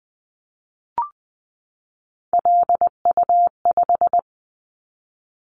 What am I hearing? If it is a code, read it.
Morse code, heard as a keyed tone, about 20 wpm, 700 Hz. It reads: LU5
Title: Callsign Prefixes for Top 50 DXCC - 20wpm